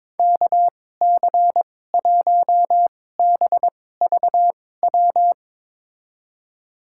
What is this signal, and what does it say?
Morse code, heard as a keyed tone, about 22 wpm, spaced as for 16 wpm, 700 Hz. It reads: KC1BVW
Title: USA Call Signs - 6 Characters - 2x Character Spacing - 22wpm